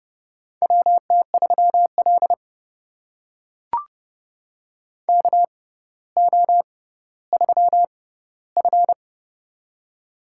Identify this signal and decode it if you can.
Morse code, heard as a keyed tone, about 30 wpm, spaced as for 15 wpm, 700 Hz. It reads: WT3L KO3F
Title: USA Call Signs - 4 Characters - 6x Character Spacing - 30wpm